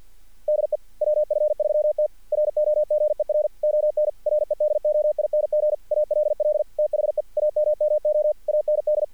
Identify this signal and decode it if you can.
Morse code, heard as a keyed tone, about 46 wpm, 600 Hz. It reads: DE WF3T ROGER ON REDOING ALL THE AMMO AND
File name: archived / CRSnet